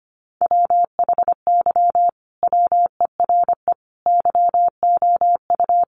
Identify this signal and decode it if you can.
Morse code, heard as a keyed tone, about 25 wpm, 700 Hz. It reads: WHY WERE YOU
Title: Sentences from Top 1400 Words 25wpm